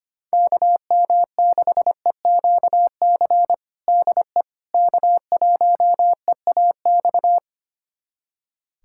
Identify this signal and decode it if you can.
Morse code, heard as a keyed tone, about 25 wpm, 700 Hz. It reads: KM6EQC DE K1EAX
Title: USA Callsign DE Callsign 25wpm